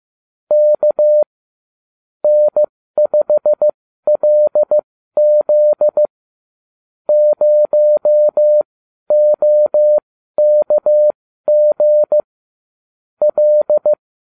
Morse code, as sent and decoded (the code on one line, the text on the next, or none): -.- / -. ..... .-.. --.. / ----- --- -.- --. / .-..
K N5LZ 0OKG L